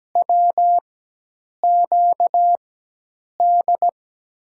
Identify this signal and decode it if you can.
Morse code, heard as a keyed tone, about 17 wpm, spaced as for 8 wpm, 700 Hz. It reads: WQD
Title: USA Call Signs - 6 Characters - 4x Character Spacing - 17wpm